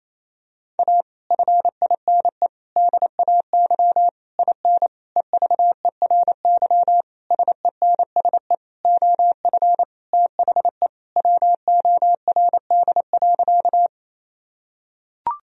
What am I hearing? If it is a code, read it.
Morse code, heard as a keyed tone, about 28 wpm, 700 Hz. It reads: A FINE DAY IN EVERY SENSE OF THE WORD.
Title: Sentences from Top 1000 Words - Review Part 4 - 28wpm